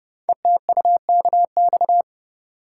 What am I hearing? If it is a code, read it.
Morse code, heard as a keyed tone, about 29 wpm, 700 Hz. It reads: ETUKX
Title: Code Groups - 5 Letters - 30wpm